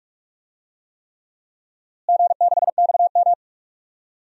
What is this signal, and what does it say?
Morse code, heard as a keyed tone, about 47 wpm, 700 Hz. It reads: G6XM
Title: DXCC-100 Call Signs - 4 Characters - Repeated 3x - 45wpm